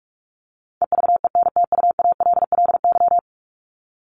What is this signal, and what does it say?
Morse code, heard as a keyed tone, about 44 wpm, 700 Hz. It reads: EVENTUALLY
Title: LY Suffix - Top 100 Words with Base Words 45wpm